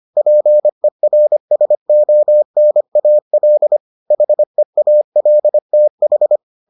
Morse code, heard as a keyed tone, about 25 wpm, 600 Hz. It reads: PERSONAL HEALTH